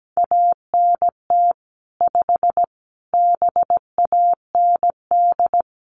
Morse code, heard as a keyed tone, about 17 wpm, 700 Hz. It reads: ANT 5 BAND
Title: Common QSO Phrases 17wpm